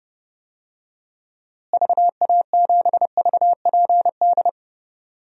Morse code, heard as a keyed tone, about 32 wpm, 700 Hz. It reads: VA7VPD
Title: DXCC-100 Call Signs - 6 Characters - Repeated 3x - 30wpm